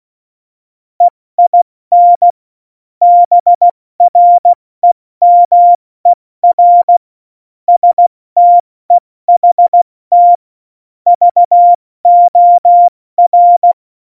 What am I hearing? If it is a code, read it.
Morse code, heard as a keyed tone, about 16 wpm, spaced as for 14 wpm, 700 Hz. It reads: EIN BREMER STEHT VOR
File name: Morsekurs / Witze / Witz 05. Januar 2021, 70 F80